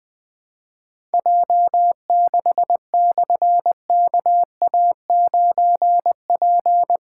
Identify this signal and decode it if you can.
Morse code, heard as a keyed tone, about 20 wpm, 700 Hz. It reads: J6/KA9P
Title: Difficult Call Signs 20wpm